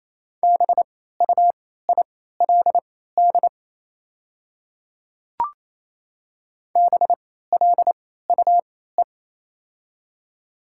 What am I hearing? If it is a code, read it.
Morse code, heard as a keyed tone, about 28 wpm, spaced as for 16 wpm, 700 Hz. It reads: BUILD BLUE
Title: 3x Character Spacing - Top 500 Words - 28wpm